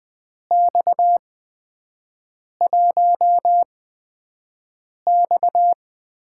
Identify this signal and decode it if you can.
Morse code, heard as a keyed tone, about 20 wpm, spaced as for 5 wpm, 700 Hz. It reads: X1X
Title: USA Call Signs - 4 Characters - 8x Character Spacing - 20wpm